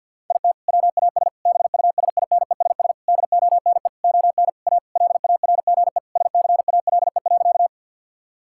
Morse code, heard as a keyed tone, about 50 wpm, 700 Hz. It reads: IT WAS BUSINESS DONE ON A LARGE SCALE.
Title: Sentences from Top 1000 Words - Review Part 5 - 50wpm